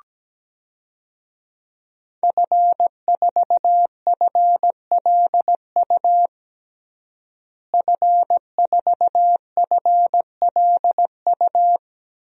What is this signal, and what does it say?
Morse code, heard as a keyed tone, about 17 wpm, 700 Hz. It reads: F4FLU F4FLU
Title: DXCC-100 Call Signs - 5 Characters - Repeated 3x - 17wpm